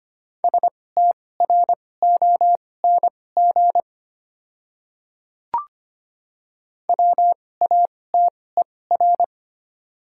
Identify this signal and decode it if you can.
Morse code, heard as a keyed tone, about 25 wpm, spaced as for 18 wpm, 690 Hz. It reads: STRONG WATER
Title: 2x Character Spacing - Top 500 Words - 25wpm